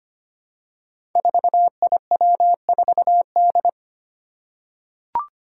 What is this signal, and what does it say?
Morse code, heard as a keyed tone, about 25 wpm, 700 Hz. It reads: 4IW4D
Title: Five Challenging Letter-Number Combinations 25wpm